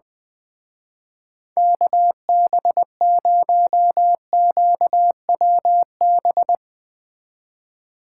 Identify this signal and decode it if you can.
Morse code, heard as a keyed tone, about 20 wpm, 700 Hz. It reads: KB0QWB
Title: Speed Racing - USA 2x3 Call Signs 20wpm